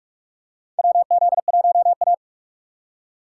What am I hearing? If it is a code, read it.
Morse code, heard as a keyed tone, about 47 wpm, 700 Hz. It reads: WZ1A
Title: USA Call Signs - 4 Characters - 2x Character Spacing - 45wpm